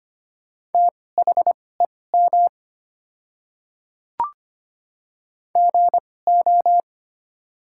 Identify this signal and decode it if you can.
Morse code, heard as a keyed tone, about 25 wpm, spaced as for 18 wpm, 700 Hz. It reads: THEM GO
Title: 2x Character Spacing - Top 100 Words - 25wpm